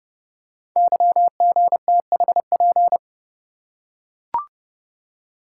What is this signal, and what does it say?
Morse code, heard as a keyed tone, about 29 wpm, 700 Hz. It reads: YGTHP
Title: Code Groups - 5 Letters - 30wpm